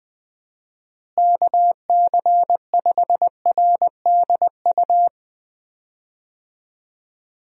Morse code, heard as a keyed tone, about 20 wpm, 700 Hz. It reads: KC5RDU
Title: USA Call Signs - 6 Characters - Repeated 3x - 20wpm